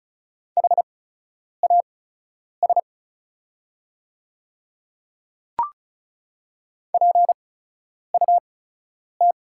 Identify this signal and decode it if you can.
Morse code, heard as a keyed tone, about 36 wpm, spaced as for 10 wpm, 690 Hz. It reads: HAS PUT
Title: 8x Character Spacing - Top 200 Words - 35wpm